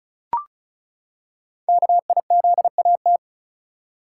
Morse code, heard as a keyed tone, about 33 wpm, 700 Hz. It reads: KIZAT
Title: Code Groups - 5 Letters - 35wpm